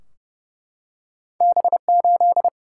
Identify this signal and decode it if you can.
Morse code, heard as a keyed tone, about 30 wpm, 700 Hz. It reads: B8
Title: Single-Letter Single-Number 30wpm